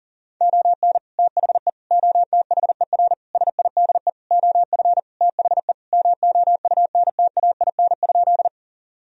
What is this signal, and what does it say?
Morse code, heard as a keyed tone, about 37 wpm, 700 Hz. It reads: ON THE OTHER SIDE OF THE MOUNTAIN?
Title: Sentences from Top 1000 Words - Review Part 1 - 40wpm